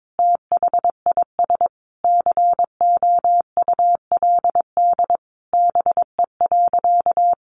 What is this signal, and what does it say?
Morse code, heard as a keyed tone, about 22 wpm, 700 Hz. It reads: THIS COULD BE.
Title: Sentences from Top 1400 Words 22wpm